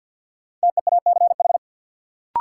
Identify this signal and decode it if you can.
Morse code, heard as a keyed tone, about 47 wpm, 700 Hz. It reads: TEACH
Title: Sentences from Top 500 Words - New Word Drill - 50wpm